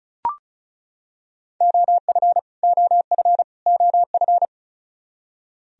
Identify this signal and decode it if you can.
Morse code, heard as a keyed tone, about 34 wpm, 700 Hz. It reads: OF OF OF
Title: Top 100 Words Repeated 35wpm